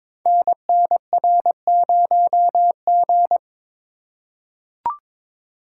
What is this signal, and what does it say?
Morse code, heard as a keyed tone, about 22 wpm, 700 Hz. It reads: NNR0G
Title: Code Groups - 5 Letter:Numbers - 22wpm